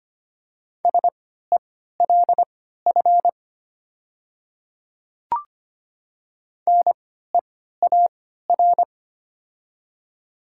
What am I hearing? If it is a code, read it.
Morse code, heard as a keyed tone, about 25 wpm, spaced as for 14 wpm, 700 Hz. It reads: SELF NEAR
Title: 3x Character Spacing - Top 500 Words - 25wpm